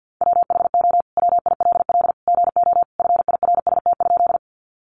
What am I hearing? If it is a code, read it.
Morse code, heard as a keyed tone, about 50 wpm, 700 Hz. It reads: WHO WILL GO FIRST?
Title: Sentences from Top 100 Words 50wpm